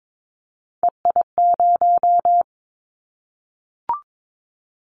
Morse code, heard as a keyed tone, about 22 wpm, 700 Hz. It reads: EI0
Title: Callsign Prefixes for Top 50 DXCC - 22wpm